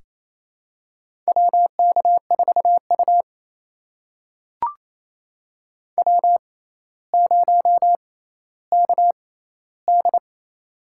Morse code, heard as a keyed tone, about 28 wpm, spaced as for 14 wpm, 700 Hz. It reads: WK4U W0KD
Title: USA Call Signs - 4 Characters - 6x Character Spacing - 28wpm